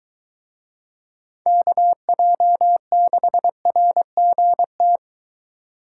Morse code, heard as a keyed tone, about 22 wpm, 700 Hz. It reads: KJ6RGT